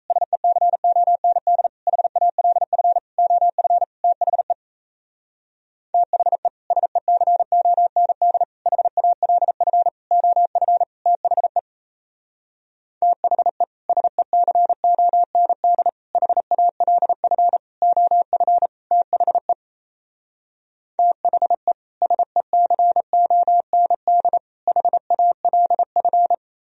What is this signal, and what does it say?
Morse code, heard as a keyed tone, about 37 wpm, 700 Hz. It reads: SECOND HALF OF THE THE SECOND HALF OF THE THE SECOND HALF OF THE THE SECOND HALF